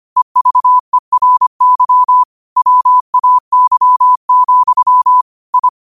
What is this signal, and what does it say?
Morse code, heard as a keyed tone, about 25 wpm, 1000 Hz. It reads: EVERY WAY, I
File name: qotdfiles / 25wpm